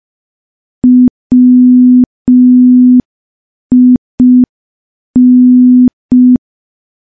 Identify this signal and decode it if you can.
Morse code, heard as a keyed tone, about 5 wpm, 250 Hz. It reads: WIN
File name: morse / morse-training-05WPM